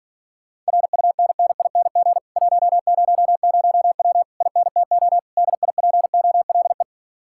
Morse code, heard as a keyed tone, about 50 wpm, 700 Hz. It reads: RUNNING 100W INTO DIPOLE
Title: Speed-Racing Common QSO Phrases - 40wpm